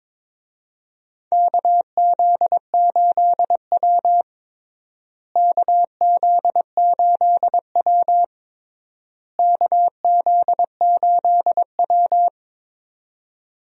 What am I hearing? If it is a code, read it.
Morse code, heard as a keyed tone, about 22 wpm, 700 Hz. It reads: KZ8W KZ8W KZ8W
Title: USA Call Signs - 4 Characters - Repeated 3x - 22wpm